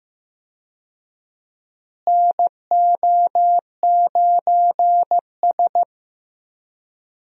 Morse code, heard as a keyed tone, about 15 wpm, 700 Hz. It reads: NO9S
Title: USA Call Signs - 4 Characters - Repeated 3x - 15wpm